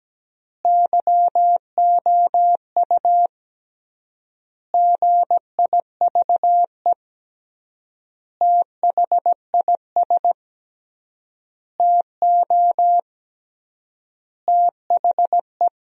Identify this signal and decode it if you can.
Morse code, heard as a keyed tone, about 17 wpm, 700 Hz. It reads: YOU GIVE THIS TO THE
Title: 3x Word-Spacing - Sentences from Top 200 Words - 17wpm